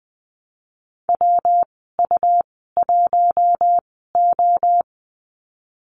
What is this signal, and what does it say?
Morse code, heard as a keyed tone, about 20 wpm, spaced as for 15 wpm, 700 Hz. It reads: WU1O
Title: 122 Lesson 20g - Number 3 - Call Signs 20wpm